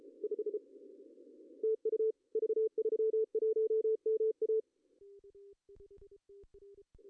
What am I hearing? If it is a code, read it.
Morse code, heard as a keyed tone, about 33 wpm, 440 Hz. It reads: 5 TU V31MA
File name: CW / cqww-cw-k5tr